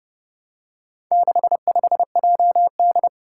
Morse code, heard as a keyed tone, about 29 wpm, 700 Hz. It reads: 65JD
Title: Four Challenging Letter-Number Combinations 30wpm